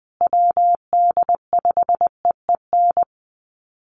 Morse code, heard as a keyed tone, about 20 wpm, 700 Hz. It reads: WD5EEN
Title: USA Call Signs Repeated 3x 20wpm